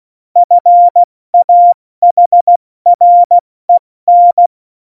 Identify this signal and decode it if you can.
Morse code, heard as a keyed tone, about 16 wpm, spaced as for 14 wpm, 700 Hz. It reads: FAHREN
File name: Morsekurs / Witze / Witz 22. Mai 2018, 70 F80